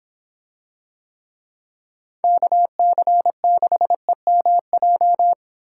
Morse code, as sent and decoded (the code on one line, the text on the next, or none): -.- -.-. -.... -- .---
KC6EMJ